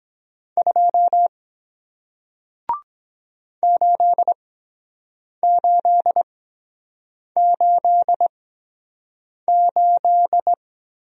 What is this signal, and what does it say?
Morse code, heard as a keyed tone, about 23 wpm, spaced as for 7 wpm, 700 Hz. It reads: 2 8888